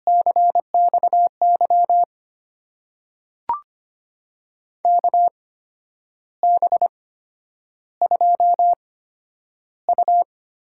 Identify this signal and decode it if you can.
Morse code, heard as a keyed tone, about 25 wpm, spaced as for 10 wpm, 700 Hz. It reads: CXY KB2U
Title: USA Call Signs - 6 Characters - 8x Character Spacing - 25wpm